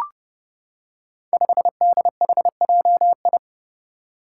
Morse code, heard as a keyed tone, about 31 wpm, 700 Hz. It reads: 5DHJI
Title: Five Challenging Letter-Number Combinations 30wpm